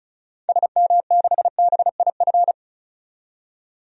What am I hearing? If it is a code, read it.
Morse code, heard as a keyed tone, about 36 wpm, 700 Hz. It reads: SM6BIF
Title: DXCC-100 Call Signs - 6 Characters - 35wpm